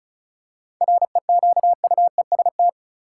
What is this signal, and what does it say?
Morse code, heard as a keyed tone, about 34 wpm, 700 Hz. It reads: REQUEST
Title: Seven Letter Words - 35wpm